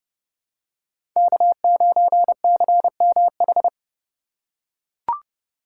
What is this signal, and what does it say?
Morse code, heard as a keyed tone, about 30 wpm, 700 Hz. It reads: K9CMH
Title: USA Call Signs - 8x Character Spacing - 30wpm